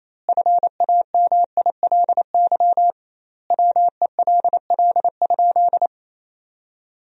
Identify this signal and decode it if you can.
Morse code, heard as a keyed tone, about 28 wpm, 700 Hz. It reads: FAMILY WELL?
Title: 2x Word-Spacing - Sentences from Top 400 Words - 28wpm